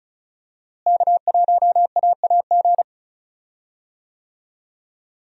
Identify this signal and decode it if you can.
Morse code, heard as a keyed tone, about 34 wpm, 700 Hz. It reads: K1AAG
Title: USA Call Signs - 5 Characters - Repeated 3x - 35wpm